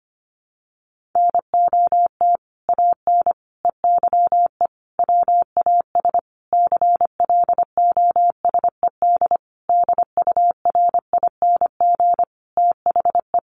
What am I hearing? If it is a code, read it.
Morse code, heard as a keyed tone, about 25 wpm, 700 Hz. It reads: NOT AN EYE WAS CLOSED DURING THE
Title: Sentences from Top 1400 Words 25wpm